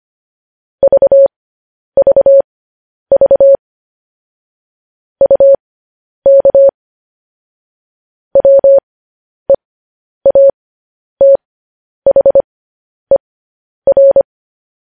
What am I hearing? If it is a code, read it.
Morse code, heard as a keyed tone, about 25 wpm, spaced as for 10 wpm, 550 Hz. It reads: VVV UK WEATHER
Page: news headlines in morse code at 10 wpm